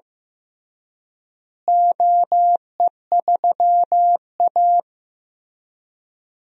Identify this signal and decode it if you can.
Morse code, heard as a keyed tone, about 15 wpm, 700 Hz. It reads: OE3A
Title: DXCC-100 Call Signs - 4 Characters - Repeated 3x - 15wpm